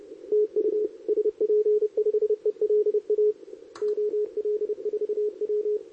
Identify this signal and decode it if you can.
Morse code, heard as a keyed tone, about 29 wpm, 440 Hz, with radio noise. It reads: TU SP5ELA OL4W
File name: CQWWCW2023 / SP